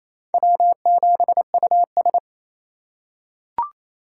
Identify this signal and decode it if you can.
Morse code, heard as a keyed tone, about 27 wpm, 700 Hz. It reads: W7US
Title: USA Call Signs - 4 Characters - 4x Character Spacing - 28wpm